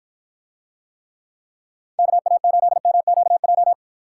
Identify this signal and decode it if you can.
Morse code, heard as a keyed tone, about 48 wpm, 700 Hz. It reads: KA8MQJ